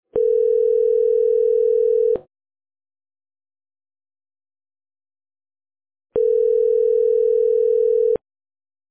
Ringback tone (North America), 440 + 480 Hz.